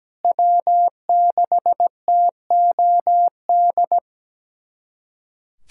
Morse code, heard as a keyed tone, about 17 wpm, 700 Hz. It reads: W6TOD
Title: USA Callsign DE Callsign - 5 Characters - 17wpm